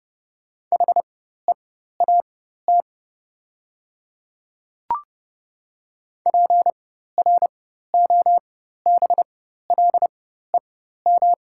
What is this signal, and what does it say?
Morse code, heard as a keyed tone, about 30 wpm, spaced as for 14 wpm, 700 Hz. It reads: HEAT PROBLEM
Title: 4x Character Spacing - Top 500 Words - 30wpm